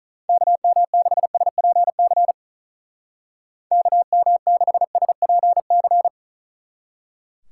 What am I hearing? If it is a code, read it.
Morse code, heard as a keyed tone, about 45 wpm, 700 Hz. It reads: KM6SPC KM6SPC